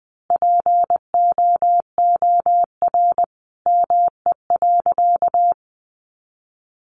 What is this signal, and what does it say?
Morse code, heard as a keyed tone, about 19 wpm, 700 Hz. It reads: POOR ME.
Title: Sentences from Top 1000 Words 20wpm